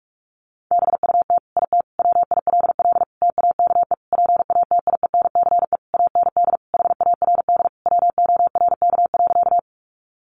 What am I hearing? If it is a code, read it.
Morse code, heard as a keyed tone, about 46 wpm, 700 Hz. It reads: BUT IT WILL TAKE PATIENCE AND HARD WORK.
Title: Sentences from Top 1300 Words 45wpm